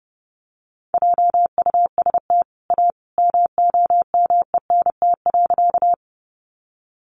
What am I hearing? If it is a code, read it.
Morse code, heard as a keyed tone, about 30 wpm, 700 Hz. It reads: JUST A MOMENT.
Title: Sentences from Top 1000 Words 30wpm